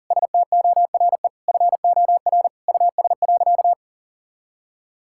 Morse code, heard as a keyed tone, about 40 wpm, 700 Hz. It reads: STORE FOR US.
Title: Sentences from Top 1000 Words - Review Part 5 - 40wpm